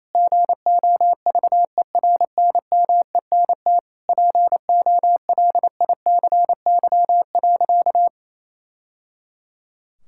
Morse code, heard as a keyed tone, about 27 wpm, 700 Hz. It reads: GOVERNMENT POLICY.
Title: Sentences from Top 1000 Words - Review Part 5 - 28wpm